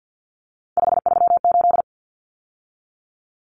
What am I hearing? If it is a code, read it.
Morse code, heard as a keyed tone, about 50 wpm, 700 Hz. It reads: 538
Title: Common Cut Numbers - 3-Digit 50wpm